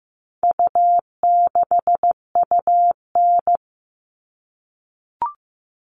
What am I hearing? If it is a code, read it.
Morse code, heard as a keyed tone, about 15 wpm, 700 Hz. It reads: U6UN
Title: All Cut Numbers - 4-Digit 15wpm